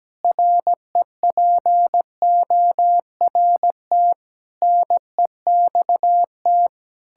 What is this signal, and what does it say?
Morse code, heard as a keyed tone, about 17 wpm, 700 Hz. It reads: REPORT NEXT
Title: Sentences from Top 1000 Words - Review Part 6 - 17wpm